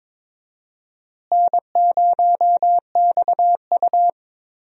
Morse code, heard as a keyed tone, about 22 wpm, 700 Hz. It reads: N0XU
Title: USA Call Signs - 4 Characters - Repeated 2x - 22wpm